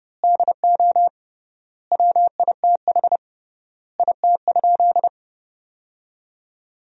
Morse code, heard as a keyed tone, about 29 wpm, 700 Hz. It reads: DO WITH IT?
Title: 3x Word-Spacing - Sentences from Top 100 Words - 30wpm